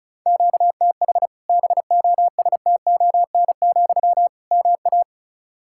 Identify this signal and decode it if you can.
Morse code, heard as a keyed tone, about 36 wpm, 700 Hz. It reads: QTH BOSTON, MA
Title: Practice QSO - Exchange 1 - Beginning - 35wpm